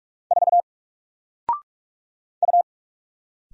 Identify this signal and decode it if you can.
Morse code, heard as a keyed tone, about 44 wpm, 700 Hz. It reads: 4 U
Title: Single Letter-Number Alt 45wpm